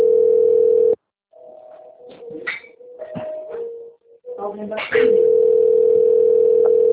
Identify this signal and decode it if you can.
Ringback tone (North America), 440 + 480 Hz, with speech.
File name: out- - - - - .30043